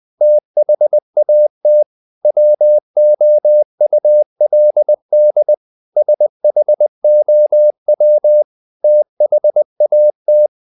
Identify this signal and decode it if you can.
Morse code, heard as a keyed tone, about 20 wpm, 600 Hz. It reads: THAT WOULD SHOW THAT
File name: the old man and the sea / the old man and the sea part 1 20wpm